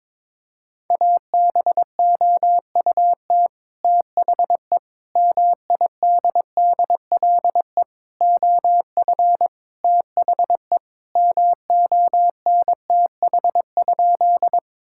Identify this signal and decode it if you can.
Morse code, heard as a keyed tone, about 22 wpm, 700 Hz. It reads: ABOUT THE MIDDLE OF THE MONTH?
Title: Sentences from Top 1000 Words - Review Part 6 - 22wpm